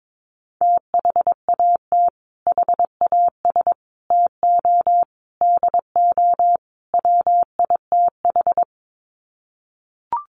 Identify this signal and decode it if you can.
Morse code, heard as a keyed tone, about 22 wpm, 700 Hz. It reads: THAT HAS TO DO WITH
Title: Sets of 5 Words Encore2 22wpm